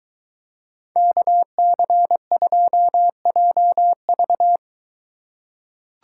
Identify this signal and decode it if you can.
Morse code, heard as a keyed tone, about 22 wpm, 700 Hz. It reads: KC2JV